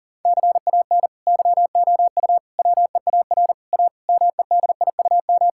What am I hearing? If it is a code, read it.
Morse code, heard as a keyed tone, about 40 wpm, 690 Hz. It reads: CAN YOU WEAR A MEDIUM
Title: Sentences from Top 1000 Words - Review Part 6 - 40wpm